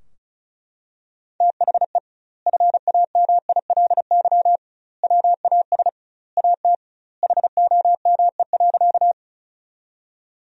Morse code, heard as a keyed tone, about 36 wpm, 700 Hz. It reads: THE FAMILY WAS AT HOME.
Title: 2x Word-Spacing - Sentences from Top 400 Words - 35wpm